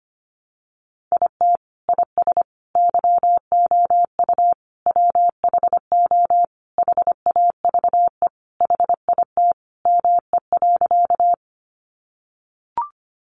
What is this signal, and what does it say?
Morse code, heard as a keyed tone, about 25 wpm, 700 Hz. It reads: IT IS YOU WHO HAVE HIT ME.
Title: Sentences from Top 1200 Words 25wpm